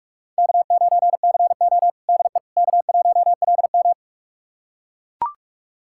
Morse code, heard as a keyed tone, about 42 wpm, 700 Hz. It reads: K9CO DE K1LM
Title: USA Callsign DE Callsign - 4 Characters - 45wpm